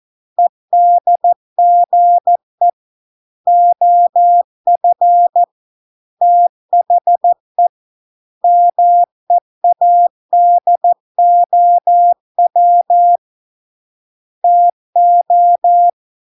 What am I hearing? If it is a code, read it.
Morse code, heard as a keyed tone, about 14 wpm, 700 Hz. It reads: EDGE OF THE MEADOW TO